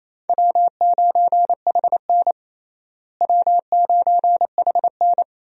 Morse code, heard as a keyed tone, about 27 wpm, 700 Hz. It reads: W9HN W9HN
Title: USA Call Signs - 4 Characters - Repeated 3x - 28wpm